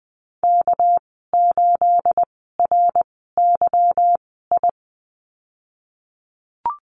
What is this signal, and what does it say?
Morse code, heard as a keyed tone, about 20 wpm, spaced as for 15 wpm, 700 Hz. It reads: K8RYI
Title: 213 Lesson 34g - Number 8 - Call Signs 20wpm